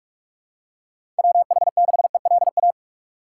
Morse code, heard as a keyed tone, about 45 wpm, 700 Hz. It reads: WH6ELA
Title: USA Call Signs - 6 Characters - 2x Character Spacing - 45wpm